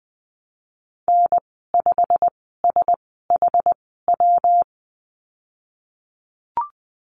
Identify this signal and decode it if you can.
Morse code, heard as a keyed tone, about 20 wpm, spaced as for 15 wpm, 700 Hz. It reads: N5SHW
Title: 122 Lesson 20g - Number 3 - Call Signs 20wpm